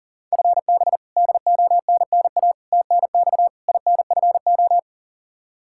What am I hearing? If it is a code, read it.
Morse code, heard as a keyed tone, about 40 wpm, 700 Hz. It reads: FB DONNA TNX INFO
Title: Practice QSO - Exchange 3 - More Info - 40wpm